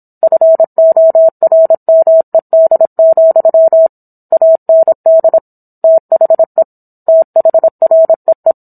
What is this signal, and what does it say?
Morse code, heard as a keyed tone, about 27 wpm, 650 Hz. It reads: FORMED, AND THE THREE